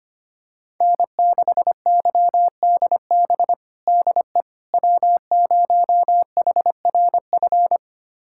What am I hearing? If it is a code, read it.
Morse code, heard as a keyed tone, about 25 wpm, 700 Hz. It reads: N6YDB DE W0HRF
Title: USA Callsign DE Callsign - 5 Characters - 25wpm